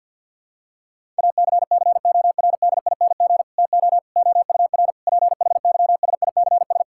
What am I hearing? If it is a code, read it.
Morse code, heard as a keyed tone, about 50 wpm, 700 Hz. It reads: ACCORDING TO OUR PHYSICS
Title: Sentences from Top 1000 Words - Review Part 5 - 50wpm